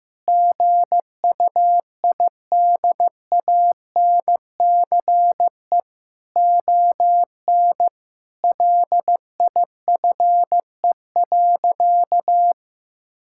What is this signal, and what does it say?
Morse code, heard as a keyed tone, about 15 wpm, 700 Hz. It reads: GUIDANCE ON LIFE.
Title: Sentences from Top 1600 Words 15wpm